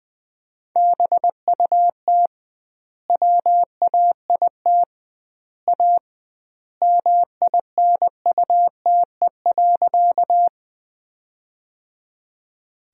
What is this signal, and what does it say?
Morse code, heard as a keyed tone, about 21 wpm, 700 Hz. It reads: BUT WAIT A MINUTE.
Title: 2x Word-Spacing - Sentences from Top 500 Words - 20wpm